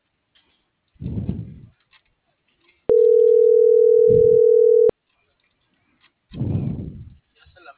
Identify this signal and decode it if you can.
Ringback tone (North America), 440 + 480 Hz.